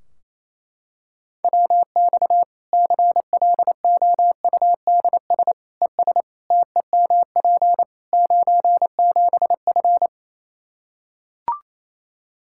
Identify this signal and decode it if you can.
Morse code, heard as a keyed tone, about 27 wpm, 700 Hz. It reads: WX CLOUDS ES TEMP 97F
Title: Practice QSO - Exchange 2 - Info - 28wpm